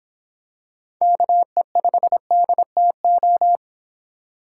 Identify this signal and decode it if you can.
Morse code, heard as a keyed tone, about 26 wpm, 700 Hz. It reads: KE5DTO